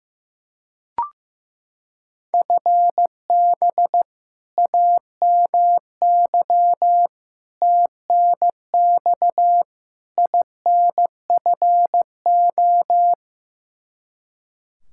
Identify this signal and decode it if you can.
Morse code, heard as a keyed tone, about 15 wpm, 700 Hz. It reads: FB AMY TNX INFO
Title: Practice QSO - Exchange 3 - More Info - 15wpm